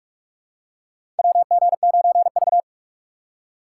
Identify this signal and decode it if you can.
Morse code, heard as a keyed tone, about 48 wpm, 700 Hz. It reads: WG9V
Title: USA Call Signs - 3x Character Spacing - 45wpm